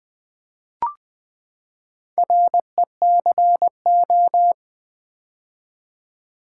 Morse code, heard as a keyed tone, about 20 wpm, 700 Hz. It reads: RECO
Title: Build A Word - 4 Letters At A Time - 11-Letter Words - 20wpm